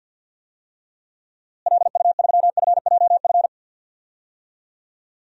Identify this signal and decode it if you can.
Morse code, heard as a keyed tone, about 52 wpm, 700 Hz. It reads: LU3FJF
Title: DXCC-100 Call Signs - 6 Characters - Repeated 2x - 50wpm